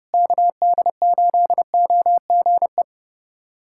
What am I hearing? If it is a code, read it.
Morse code, heard as a keyed tone, about 30 wpm, 700 Hz. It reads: KD8OGE
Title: USA Callsign DE Callsign - 6 Characters - 30wpm